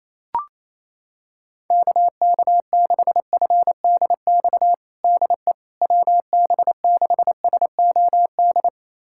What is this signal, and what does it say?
Morse code, heard as a keyed tone, about 28 wpm, 700 Hz. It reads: KK6FDX DE WB6SOD
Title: USA Callsign DE Callsign - 6 Characters - 28wpm